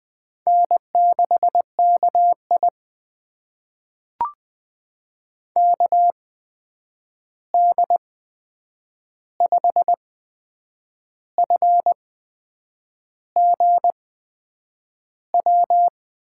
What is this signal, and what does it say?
Morse code, heard as a keyed tone, about 20 wpm, spaced as for 8 wpm, 700 Hz. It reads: N6KI KD5FGW
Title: USA Call Signs - 8x Character Spacing - 20wpm